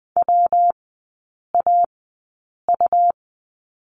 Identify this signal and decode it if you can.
Morse code, heard as a keyed tone, about 20 wpm, spaced as for 8 wpm, 700 Hz. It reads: WAU
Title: 121 Lesson 20e - Number 3 - Three Characters 20wpm